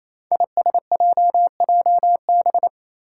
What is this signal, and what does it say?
Morse code, heard as a keyed tone, about 28 wpm, 700 Hz. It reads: ISJJB
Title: Five Challenging Letter-Number Combinations 28wpm